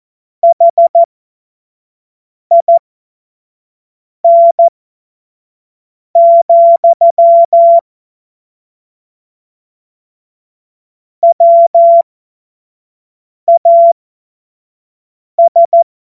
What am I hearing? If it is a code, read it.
Morse code, heard as a keyed tone, about 14 wpm, spaced as for 5 wpm, 680 Hz. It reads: HIN, WAS
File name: Morsekurs / Witze / Witz 02. Mai 2023 25 F70